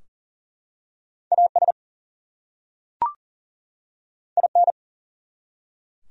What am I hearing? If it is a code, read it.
Morse code, heard as a keyed tone, about 40 wpm, 700 Hz. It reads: AS IN
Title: Top 100 Words 40wpm